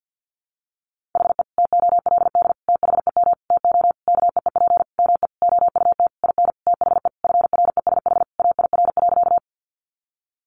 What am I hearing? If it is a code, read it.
Morse code, heard as a keyed tone, about 50 wpm, 700 Hz. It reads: HE TOLD THEM TO KEEP ME OUT IN THE FRESH AIR.
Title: Sentences from Top 1200 Words 50wpm